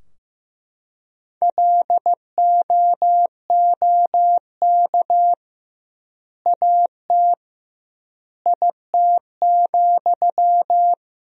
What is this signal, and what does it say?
Morse code, heard as a keyed tone, about 15 wpm, 700 Hz. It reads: LOOK AT IT,
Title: 2x Word-Spacing - Sentences from Top 200 Words - 15wpm